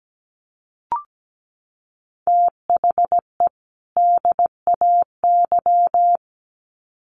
Morse code, heard as a keyed tone, about 17 wpm, 700 Hz. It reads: THE DAY
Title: Sets of 2 Words 17wpm